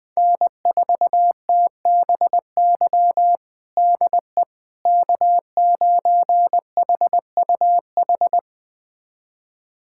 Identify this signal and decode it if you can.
Morse code, heard as a keyed tone, about 20 wpm, 700 Hz. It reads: N4TBY DE K9HUH
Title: USA Callsign DE Callsign - 5 Characters - 20wpm